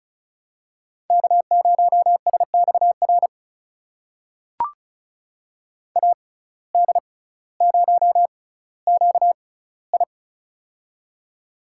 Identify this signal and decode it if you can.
Morse code, heard as a keyed tone, about 36 wpm, spaced as for 18 wpm, 700 Hz. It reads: K0SXR AD0QI
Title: USA Call Signs - 5 Characters - 6x Character Spacing - 35wpm